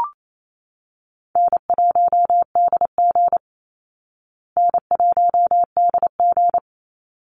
Morse code, heard as a keyed tone, about 28 wpm, 690 Hz. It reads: N1DG N1DG
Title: Call Signs Repeated 3x - 28wpm